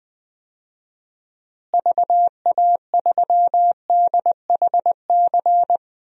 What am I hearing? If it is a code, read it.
Morse code, heard as a keyed tone, about 20 wpm, 700 Hz. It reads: VA3DHC
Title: DXCC-100 Call Signs - 6 Characters - Repeated 2x - 20wpm